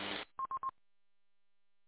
There is no code.